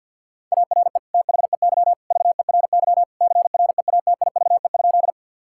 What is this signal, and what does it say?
Morse code, heard as a keyed tone, about 50 wpm, 700 Hz. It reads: ARE THEY VERY CREATIVE?
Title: Sentences from Top 1000 Words - Review Part 5 - 50wpm